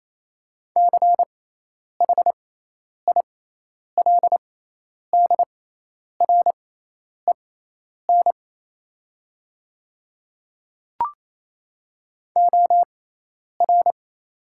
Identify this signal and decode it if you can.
Morse code, heard as a keyed tone, about 28 wpm, spaced as for 10 wpm, 700 Hz. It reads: CHILDREN OR
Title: 6x Character Spacing - Top 500 Words - 28wpm